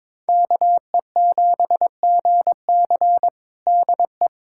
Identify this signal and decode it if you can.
Morse code, heard as a keyed tone, about 22 wpm, 700 Hz. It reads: KE7GC DE
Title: USA Callsign DE Callsign 22wpm